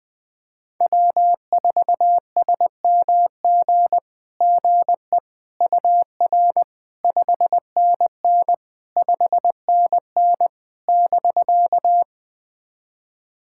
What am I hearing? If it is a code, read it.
Morse code, heard as a keyed tone, about 20 wpm, 690 Hz. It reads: W4SMG GE UR 5NN 5NN <BK>
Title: SOTA QSO Practice 20wpm